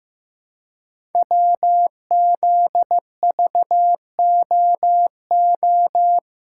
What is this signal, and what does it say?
Morse code, heard as a keyed tone, about 15 wpm, 700 Hz. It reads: WZVOO
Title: Code Groups - 5 Letter:Numbers - 15wpm